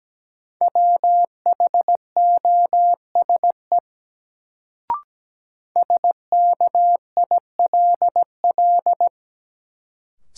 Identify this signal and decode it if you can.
Morse code, heard as a keyed tone, about 17 wpm, 700 Hz. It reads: WHOSE SKILL
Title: Sentences from Top 1200 Words - New Word Drill - 17wpm